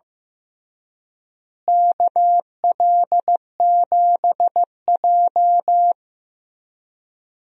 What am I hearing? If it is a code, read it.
Morse code, heard as a keyed tone, about 15 wpm, 700 Hz. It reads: KL7J
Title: DXCC-100 Call Signs - 4 Characters - Repeated 3x - 15wpm